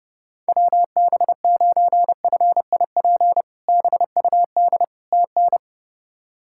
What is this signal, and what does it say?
Morse code, heard as a keyed tone, about 30 wpm, 700 Hz. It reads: WB9FIP BUD TN
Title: K1USN SST QSO Practice - 30wpm